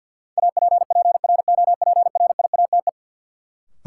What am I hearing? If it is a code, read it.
Morse code, heard as a keyed tone, about 52 wpm, 700 Hz. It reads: APPROPRIATE
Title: Sentences from Top 900 Words - New Word Drill - 50wpm